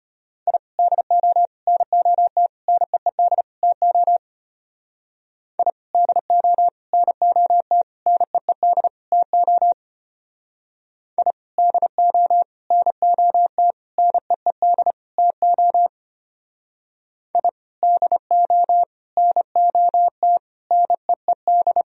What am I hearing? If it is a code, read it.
Morse code, heard as a keyed tone, about 32 wpm, 700 Hz. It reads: I DO NOT NEED TO I DO NOT NEED TO I DO NOT NEED TO I DO NOT NEED